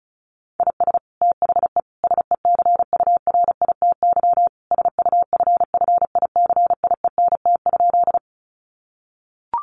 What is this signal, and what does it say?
Morse code, heard as a keyed tone, about 34 wpm, 700 Hz. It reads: IS THE SECURITY SUFFICIENT?
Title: Sentences from Top 1400 Words 35wpm